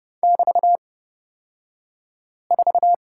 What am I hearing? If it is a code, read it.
Morse code, heard as a keyed tone, about 30 wpm, spaced as for 5 wpm, 700 Hz. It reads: <BT>4
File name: Lesson 36d - BT - Two Characters 30wpm